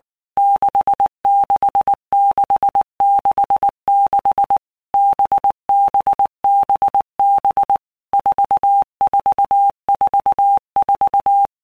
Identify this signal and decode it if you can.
Morse code, heard as a keyed tone, about 19 wpm, 800 Hz. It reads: 66666 BBBB 4444
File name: Basic Letter Groups By Blocks / Groups@18 / VB46@18